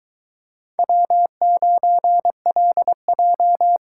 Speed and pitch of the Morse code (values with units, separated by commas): 24 wpm, 700 Hz